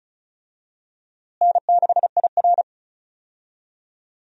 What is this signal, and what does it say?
Morse code, heard as a keyed tone, about 36 wpm, 700 Hz. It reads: N6IR
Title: USA Call Signs - 4 Characters - Repeated 2x - 35wpm